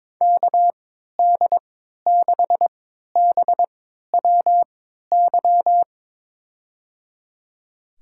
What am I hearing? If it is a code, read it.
Morse code, heard as a keyed tone, about 22 wpm, spaced as for 13 wpm, 700 Hz. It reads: KD6BWY
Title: USA Call Signs - 3x Character Spacing - 22wpm